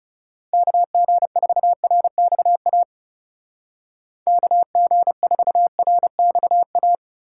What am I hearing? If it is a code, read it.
Morse code, heard as a keyed tone, about 36 wpm, 700 Hz. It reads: KG4RXA KG4RXA
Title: Speed Racing - USA Call Signs 30wpm